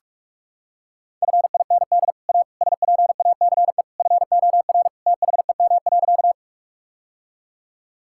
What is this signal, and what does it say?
Morse code, heard as a keyed tone, about 42 wpm, 700 Hz. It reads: FIND A SPACE FOR THEM.
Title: Sentences from Top 1000 Words - Review Part 5 - 45wpm